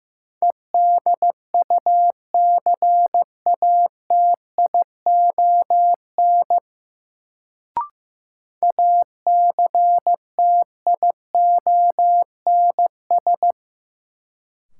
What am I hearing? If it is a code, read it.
Morse code, heard as a keyed tone, about 15 wpm, 700 Hz. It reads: EDUCATION ACTIONS
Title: Sentences from Top 1000 Words - Review Part 6 - New Word Drill - 15wpm